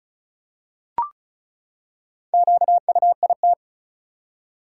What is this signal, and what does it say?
Morse code, heard as a keyed tone, about 36 wpm, 700 Hz. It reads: QUIT
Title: Four Letter Words - 35wpm